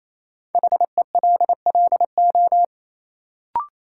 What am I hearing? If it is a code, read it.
Morse code, heard as a keyed tone, about 28 wpm, 700 Hz. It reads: HELLO
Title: Sentences from Top 1100 Words - New Word Drill Alt - 28wpm